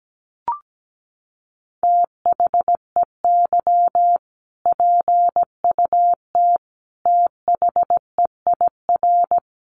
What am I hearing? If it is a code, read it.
Morse code, heard as a keyed tone, about 17 wpm, 700 Hz. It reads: THEY PUT THEIR
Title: Sentences from Top 1200 Words 17wpm